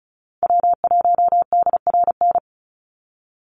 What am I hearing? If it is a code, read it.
Morse code, heard as a keyed tone, about 34 wpm, 700 Hz. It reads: W1DRN
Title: USA Call Signs 35wpm